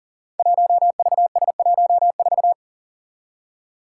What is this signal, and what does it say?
Morse code, heard as a keyed tone, about 40 wpm, 700 Hz. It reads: 1VS14
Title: Five Challenging Letter-Number Combinations 40wpm